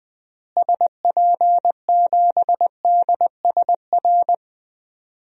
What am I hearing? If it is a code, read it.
Morse code, heard as a keyed tone, about 20 wpm, 700 Hz. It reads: SP7DSR
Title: DXCC-100 Call Signs - 6 Characters - Repeated 3x - 20wpm